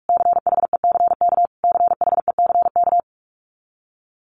Morse code, heard as a keyed tone, about 44 wpm, 700 Hz. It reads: CHECK CHECK
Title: Top 500 Words Repeated 45wpm